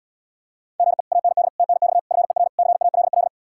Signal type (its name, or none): Morse code